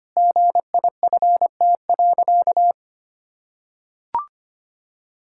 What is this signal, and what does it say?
Morse code, heard as a keyed tone, about 25 wpm, 700 Hz. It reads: GIFT.
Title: Sentences from Top 1000 Words - Review Part 6 - 25wpm